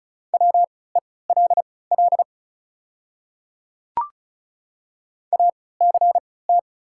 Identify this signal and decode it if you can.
Morse code, heard as a keyed tone, about 36 wpm, spaced as for 20 wpm, 700 Hz. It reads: WELL ACT
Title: 3x Character Spacing - Top 200 Words - 35wpm